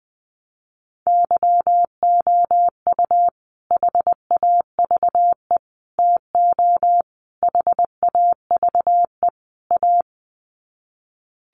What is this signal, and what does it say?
Morse code, heard as a keyed tone, about 20 wpm, 700 Hz. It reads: YOU HAVE TO HAVE A
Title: Sets of 5 Words Encore 20wpm